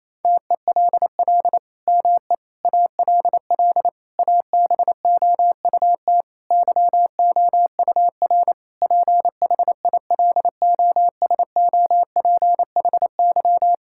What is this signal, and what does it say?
Morse code, heard as a keyed tone, about 28 wpm, 700 Hz. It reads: TELL ME ALL ABOUT YOUR PHILOSOPHY
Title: Sentences from Top 1000 Words - Review Part 6 - 28wpm